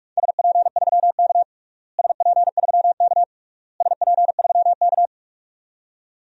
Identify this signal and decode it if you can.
Morse code, heard as a keyed tone, about 42 wpm, 700 Hz. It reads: SP3K SP3K SP3K
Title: DXCC-100 Call Signs - 4 Characters - Repeated 3x - 45wpm